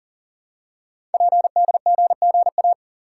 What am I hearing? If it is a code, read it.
Morse code, heard as a keyed tone, about 40 wpm, 700 Hz. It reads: PDGGA
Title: Code Groups - 5 Letters - 40wpm